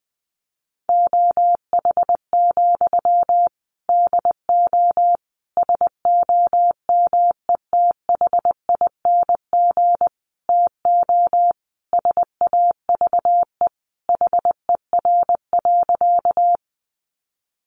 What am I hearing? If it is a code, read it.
Morse code, heard as a keyed tone, about 20 wpm, 700 Hz. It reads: OH, DO SOMETHING TO SAVE HER.
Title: Sentences from Top 1100 Words 20wpm